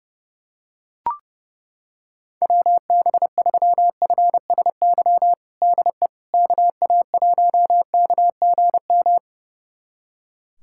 Morse code, heard as a keyed tone, about 32 wpm, 700 Hz. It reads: WB3FSY DE KA1KGM
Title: USA Callsign DE Callsign - 6 Characters - 30wpm